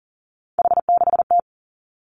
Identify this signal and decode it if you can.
Morse code, heard as a keyed tone, about 37 wpm, 700 Hz. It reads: H6T
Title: Three Challenging Letter-Number Combinations 40wpm